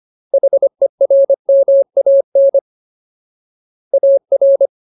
Morse code, heard as a keyed tone, about 25 wpm, 550 Hz. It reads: HERMAN AR
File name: Ultimate Word List / 25WPM Ultimate